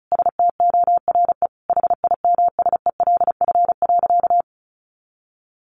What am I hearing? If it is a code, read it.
Morse code, heard as a keyed tone, about 34 wpm, 700 Hz. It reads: STORE HIMSELF.
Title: Easy Phrases 35wpm